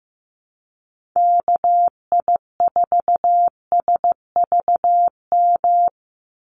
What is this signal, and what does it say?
Morse code, heard as a keyed tone, about 15 wpm, 700 Hz. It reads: KI4SVM
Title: NA SOTA Activator Callsigns 15wpm